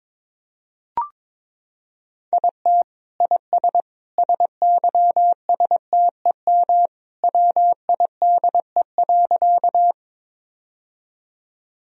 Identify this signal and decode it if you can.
Morse code, heard as a keyed tone, about 22 wpm, 700 Hz. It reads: IT IS SYSTEM WIDE.
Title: Sentences from Top 1000 Words - Review Part 3 - 22wpm